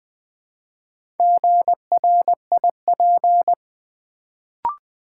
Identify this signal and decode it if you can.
Morse code, heard as a keyed tone, about 20 wpm, 690 Hz. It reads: GRIP
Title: Build A Word - 1 Letter At A Time - 4-Letter Words - 20wpm